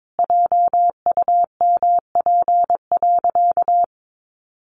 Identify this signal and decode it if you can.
Morse code, heard as a keyed tone, about 22 wpm, 700 Hz. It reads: JUMP.
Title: Sentences from Top 1300 Words 22wpm